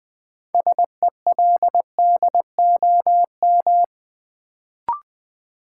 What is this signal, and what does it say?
Morse code, heard as a keyed tone, about 20 wpm, 700 Hz. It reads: SELDOM
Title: Build A Word - 3 Letters At A Time - 6-Letter Words - 20wpm